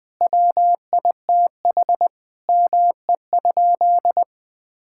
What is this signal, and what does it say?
Morse code, heard as a keyed tone, about 21 wpm, 700 Hz. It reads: WITH ME?
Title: Sentences from Top 1600 Words 20wpm